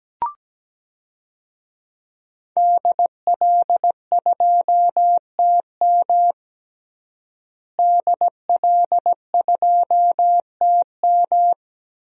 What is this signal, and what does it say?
Morse code, heard as a keyed tone, about 17 wpm, 700 Hz. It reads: DL2TM DL2TM
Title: DXCC-100 Call Signs - 5 Characters - Repeated 3x - 17wpm